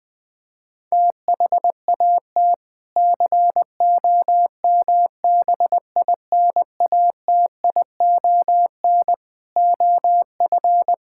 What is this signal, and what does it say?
Morse code, heard as a keyed tone, about 20 wpm, 700 Hz. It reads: THAT COMBINATION OF
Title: Sentences from Top 1000 Words - Review Part 6 - 20wpm